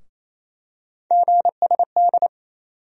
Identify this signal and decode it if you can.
Morse code, heard as a keyed tone, about 28 wpm, 700 Hz. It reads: GSD
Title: USA Call Sign Postfixes - 3 Letters 28wpm